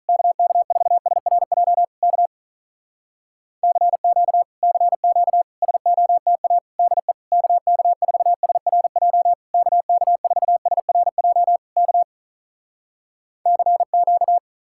Morse code, heard as a keyed tone, about 46 wpm, 700 Hz. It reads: KK4SRJ K CQ CQ SOTA DE KK4SRJ KK4SRJ K CQ